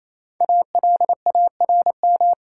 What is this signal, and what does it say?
Morse code, heard as a keyed tone, about 28 wpm, 700 Hz. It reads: ALARM
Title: Five Letter Words - 28wpm